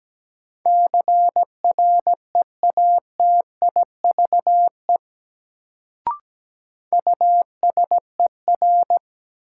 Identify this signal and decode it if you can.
Morse code, heard as a keyed tone, about 17 wpm, 700 Hz. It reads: CREATIVE USER
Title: Sentences from Top 1000 Words - Review Part 5 - New Word Drill - 17wpm